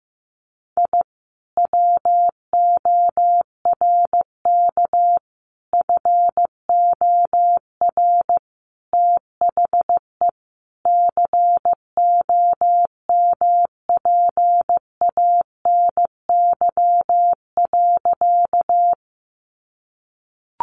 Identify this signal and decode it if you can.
Morse code, heard as a keyed tone, about 15 wpm, 700 Hz. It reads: I WORK FOR THE COMPANY.
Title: Sentences from Top 600 Words 15wpm